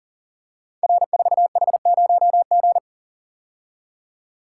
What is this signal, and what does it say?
Morse code, heard as a keyed tone, about 39 wpm, 690 Hz. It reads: R4H0G
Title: Code Groups - 5 Letter:Numbers - 40wpm